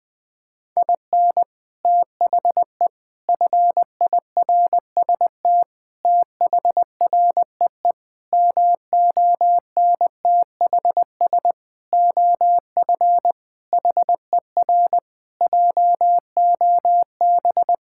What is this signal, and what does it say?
Morse code, heard as a keyed tone, about 20 wpm, 700 Hz. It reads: IN THE FIRST THREE MONTHS OF HER JOB
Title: Sentences with 6-Word Phrases - 20wpm